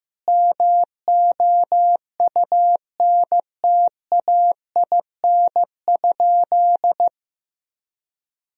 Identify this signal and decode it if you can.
Morse code, heard as a keyed tone, about 16 wpm, 700 Hz. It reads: MOUNTAIN?
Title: Sentences from Top 1000 Words - Review Part 1 - 15wpm